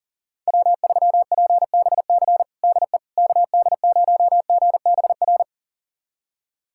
Morse code, heard as a keyed tone, about 40 wpm, 700 Hz. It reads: W3PBC DE KD0GBR
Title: USA Callsign DE Callsign 40wpm